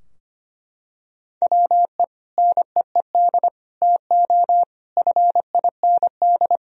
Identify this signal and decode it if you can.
Morse code, heard as a keyed tone, about 25 wpm, 700 Hz. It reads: WE NEED TO FIND
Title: Sentences from Top 1000 Words - Review Part 3 - 25wpm